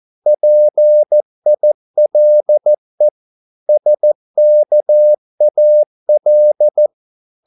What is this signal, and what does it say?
Morse code, heard as a keyed tone, about 14 wpm, 600 Hz. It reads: PILE SKAL
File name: Korte ord DK 14wpm